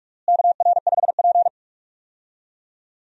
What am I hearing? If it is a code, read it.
Morse code, heard as a keyed tone, about 46 wpm, 700 Hz. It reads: KR5P
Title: USA Call Signs - 4 Characters - Repeated 3x - 45wpm